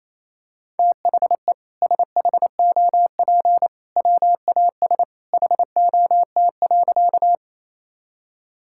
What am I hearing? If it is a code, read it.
Morse code, heard as a keyed tone, about 28 wpm, 700 Hz. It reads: THE SHOP WAS HOT.
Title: Sentences from Top 1600 Words 28wpm